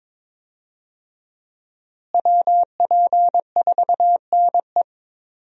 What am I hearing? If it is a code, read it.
Morse code, heard as a keyed tone, about 22 wpm, 700 Hz. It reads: WP4NE
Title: USA Call Signs - 5 Characters - Repeated 2x - 22wpm